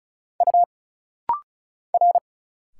Morse code, heard as a keyed tone, about 36 wpm, 700 Hz. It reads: U R
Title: Single Letter-Number Alt - Rapid-Fire 35wpm